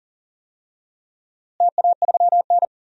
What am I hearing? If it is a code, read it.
Morse code, heard as a keyed tone, about 39 wpm, 690 Hz. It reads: TA3N
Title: DXCC-100 Call Signs - 4 Characters - Repeated 2x - 40wpm